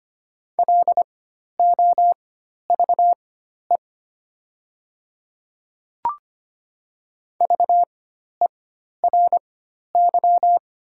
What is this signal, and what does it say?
Morse code, heard as a keyed tone, about 25 wpm, spaced as for 12 wpm, 690 Hz. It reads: LOVE VERY
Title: 4x Character Spacing - Top 500 Words - 25wpm